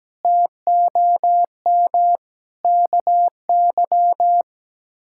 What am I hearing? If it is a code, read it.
Morse code, heard as a keyed tone, about 17 wpm, 700 Hz. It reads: TOM KY
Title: K1USN SST QSO Practice - 17wpm